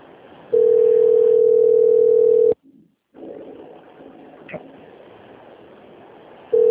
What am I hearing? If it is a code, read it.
Ringback tone (North America), 440 + 480 Hz.
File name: out- - - - - .25302